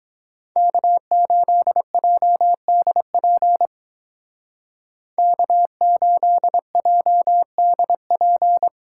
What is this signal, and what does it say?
Morse code, heard as a keyed tone, about 20 wpm, 700 Hz. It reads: K8JDP K8JDP
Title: Speed Racing - USA Call Signs 17wpm